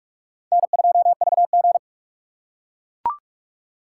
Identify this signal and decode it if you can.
Morse code, heard as a keyed tone, about 39 wpm, 700 Hz. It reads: N2VG